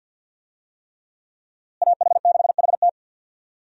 Morse code, heard as a keyed tone, about 48 wpm, 700 Hz. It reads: AH6HT
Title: USA Call Signs - 5 Characters - Repeated 3x - 50wpm